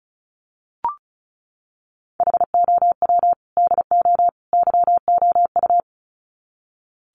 Morse code, heard as a keyed tone, about 34 wpm, 700 Hz. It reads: HOW DO YOU
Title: Sets of 3 Words 35wpm